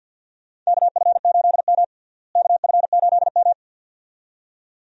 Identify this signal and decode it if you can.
Morse code, heard as a keyed tone, about 52 wpm, 700 Hz. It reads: KF8M KF8M
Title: USA Call Signs - 4 Characters - Repeated 2x - 50wpm